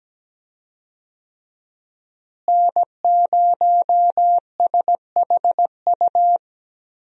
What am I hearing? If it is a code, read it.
Morse code, heard as a keyed tone, about 17 wpm, 700 Hz. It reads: N0SHU
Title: USA Call Signs - 5 Characters - Repeated 2x - 17wpm